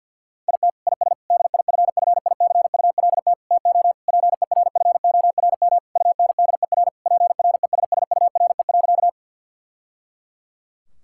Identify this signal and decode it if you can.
Morse code, heard as a keyed tone, about 50 wpm, 700 Hz. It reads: IT IS DIFFICULT TO PERFORM UNDER PRESSURE.
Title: Sentences from Top 1000 Words - Review Part 6 - 50wpm